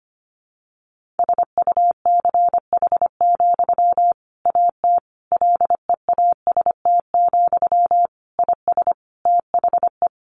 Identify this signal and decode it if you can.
Morse code, heard as a keyed tone, about 25 wpm, 700 Hz. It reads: SUCH, AT LEAST, IS THE
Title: Sentences from Top 1500 Words 25wpm